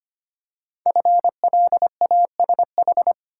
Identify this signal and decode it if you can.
Morse code, heard as a keyed tone, about 25 wpm, 700 Hz. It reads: FLASH
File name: 5 Letter Words - First 3 Letters Given - 25wpm